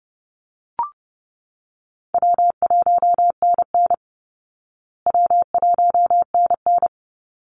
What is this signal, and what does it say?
Morse code, heard as a keyed tone, about 30 wpm, 700 Hz. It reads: W1NN W1NN
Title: Call Signs Repeated 3x - 30wpm